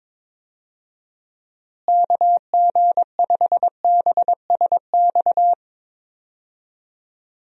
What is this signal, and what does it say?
Morse code, heard as a keyed tone, about 22 wpm, 700 Hz. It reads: KG5BSX
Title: USA Call Signs - 6 Characters - Repeated 3x - 22wpm